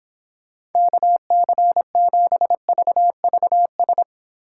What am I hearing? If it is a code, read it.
Morse code, heard as a keyed tone, about 25 wpm, 700 Hz. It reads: KC7VVS